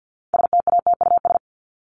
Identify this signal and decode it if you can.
Morse code, heard as a keyed tone, about 50 wpm, 700 Hz. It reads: STATUS
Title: Top 1251-1500 Words 50wpm